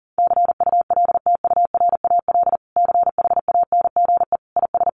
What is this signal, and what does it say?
Morse code, heard as a keyed tone, about 40 wpm, 700 Hz. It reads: CULTURAL CHANGE IS
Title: Sentences from Top 1500 Words 40wpm